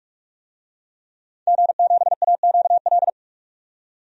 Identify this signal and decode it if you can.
Morse code, heard as a keyed tone, about 41 wpm, 700 Hz. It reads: G7AQL
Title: DXCC-100 Call Signs - 5 Characters - 45wpm